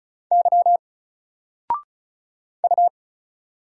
Morse code, heard as a keyed tone, about 36 wpm, 700 Hz. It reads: Y U
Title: Single Letter-Number Alt 35wpm